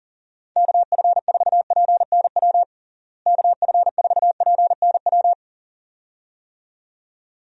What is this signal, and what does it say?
Morse code, heard as a keyed tone, about 37 wpm, 700 Hz. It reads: KF4PNW KF4PNW
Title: USA Call Signs - 6 Characters - Repeated 2x - 40wpm